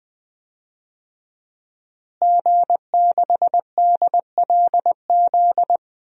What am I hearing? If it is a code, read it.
Morse code, heard as a keyed tone, about 20 wpm, 700 Hz. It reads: G6DLZ
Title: DXCC-100 Call Signs - 5 Characters - Repeated 3x - 20wpm